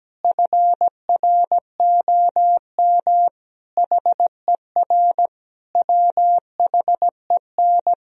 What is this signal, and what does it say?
Morse code, heard as a keyed tone, about 17 wpm, 700 Hz. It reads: FROM HER WHEN
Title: Sentences from Top 1000 Words - Review Part 3 - 17wpm